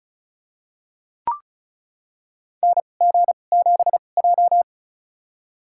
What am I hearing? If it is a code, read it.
Morse code, heard as a keyed tone, about 36 wpm, spaced as for 25 wpm, 690 Hz. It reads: NG7J
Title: USA Call Signs - 4 Characters - 2x Character Spacing - 35wpm